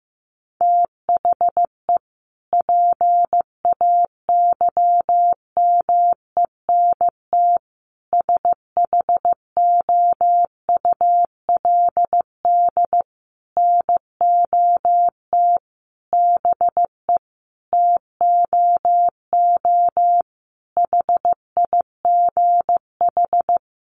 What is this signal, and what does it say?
Morse code, heard as a keyed tone, about 15 wpm, 700 Hz. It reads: THE PAYMENT SHOULD NOT BE TOO HIGH
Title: Sentences from Top 1200 Words 15wpm